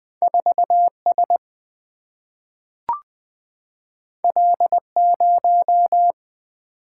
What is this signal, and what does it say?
Morse code, heard as a keyed tone, about 20 wpm, 700 Hz. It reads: 4S L0
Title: Single-Letter Single-Number 20wpm